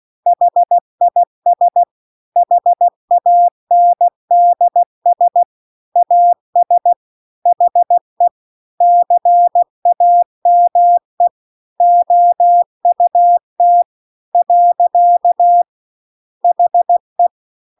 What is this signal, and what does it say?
Morse code, heard as a keyed tone, about 16 wpm, 700 Hz. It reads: HIS HANDS AS HE CAME OUT . HE
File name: 16WPM